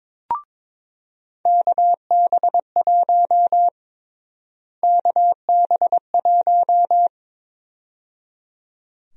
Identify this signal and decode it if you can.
Morse code, heard as a keyed tone, about 21 wpm, 700 Hz. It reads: KB1 KB1
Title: USA Call Sign Prefixes - Repeated 2x - 22wpm